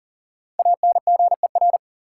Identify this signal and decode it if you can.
Morse code, heard as a keyed tone, about 38 wpm, 700 Hz. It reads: ANGER
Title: Five Letter Words - 40wpm